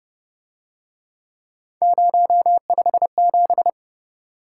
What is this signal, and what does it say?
Morse code, heard as a keyed tone, about 30 wpm, 700 Hz. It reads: 057